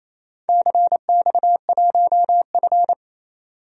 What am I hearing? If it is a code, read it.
Morse code, heard as a keyed tone, about 27 wpm, 700 Hz. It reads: CX1F
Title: DXCC-100 Call Signs - 4 Characters - 28wpm